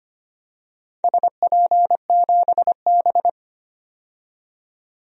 Morse code, heard as a keyed tone, about 25 wpm, 700 Hz. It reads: SP7B
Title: DXCC-100 Call Signs - 4 Characters - 25wpm